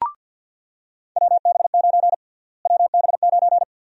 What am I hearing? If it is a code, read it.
Morse code, heard as a keyed tone, about 48 wpm, 700 Hz. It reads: WB9 WB9
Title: USA Call Sign Prefixes - Repeated 2x - 50wpm